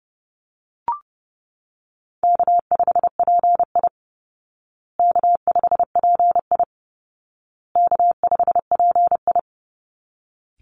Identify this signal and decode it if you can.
Morse code, heard as a keyed tone, about 29 wpm, 700 Hz. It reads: K5PI K5PI K5PI
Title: Easy Call Signs Repeated 3x - 30wpm